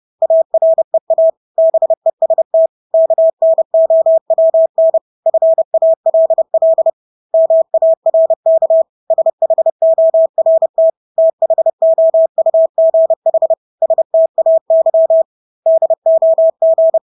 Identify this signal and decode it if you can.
Morse code, heard as a keyed tone, about 30 wpm, 650 Hz. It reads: AREA BEST KNOWN FALL MARK SHORT THOUGH STAY DOG
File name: Engelske ord 30-30 wpm